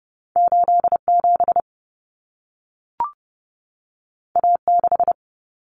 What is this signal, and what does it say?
Morse code, heard as a keyed tone, about 30 wpm, 700 Hz. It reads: 87 A6
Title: Common Cut Numbers - 2-Digit 30wpm